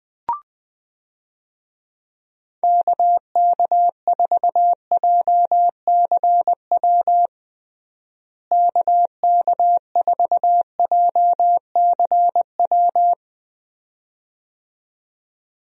Morse code, heard as a keyed tone, about 20 wpm, 700 Hz. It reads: KK4JCW KK4JCW
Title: USA Call Signs - 6 Characters - Repeated 2x - 20wpm